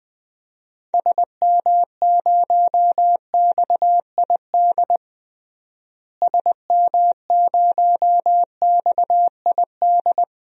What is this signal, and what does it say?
Morse code, heard as a keyed tone, about 21 wpm, 700 Hz. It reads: SM0XID SM0XID
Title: DXCC-100 Call Signs - 6 Characters - Repeated 2x - 20wpm